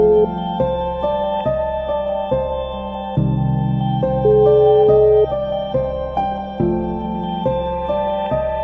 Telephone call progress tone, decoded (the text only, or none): ringback tone